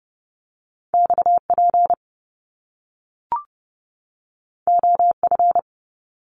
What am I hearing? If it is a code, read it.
Morse code, heard as a keyed tone, about 30 wpm, 700 Hz. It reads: XP OF
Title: Top Two Letter Combinations in Words 30wpm